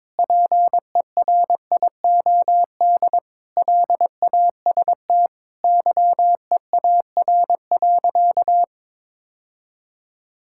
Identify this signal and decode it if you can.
Morse code, heard as a keyed tone, about 22 wpm, 700 Hz. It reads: PERIOD LAST YEAR.
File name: Sentences with 7-Word Phrases - 22wpm